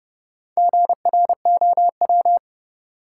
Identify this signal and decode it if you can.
Morse code, heard as a keyed tone, about 30 wpm, 700 Hz. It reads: GROW
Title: Sentences from Top 300 Words - New Word Drill - 30wpm